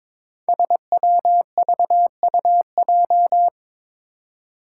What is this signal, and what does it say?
Morse code, heard as a keyed tone, about 22 wpm, 700 Hz. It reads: SWVUJ
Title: Five Challenging Letter-Number Combinations 22wpm